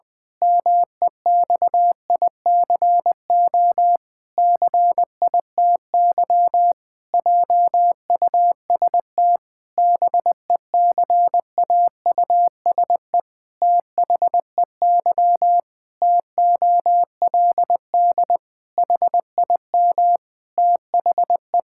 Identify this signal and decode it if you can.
Morse code, heard as a keyed tone, about 20 wpm, 700 Hz. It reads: MEXICO CITY JUST BECAUSE THEY TOLD HIM THE
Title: Pangram Challenge v2 - 20wpm